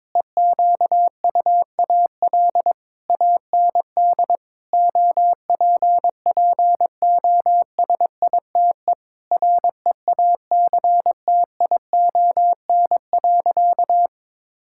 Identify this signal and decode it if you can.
Morse code, heard as a keyed tone, about 22 wpm, 700 Hz. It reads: EQUAL AND OPPOSITE REACTION.
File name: practice / 22wpm / Sentences with 8-Word Phrases - 22wpm